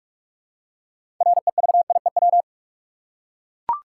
Morse code, heard as a keyed tone, about 42 wpm, 700 Hz. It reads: REVIEW
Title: Six Letter Words - 45wpm